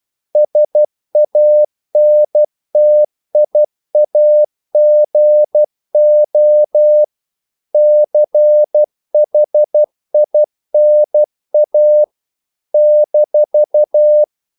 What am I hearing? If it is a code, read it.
Morse code, heard as a keyed tone, about 12 wpm, 600 Hz. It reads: SANTIAGO CHINA -
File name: CW / Practice / NationalCapitals12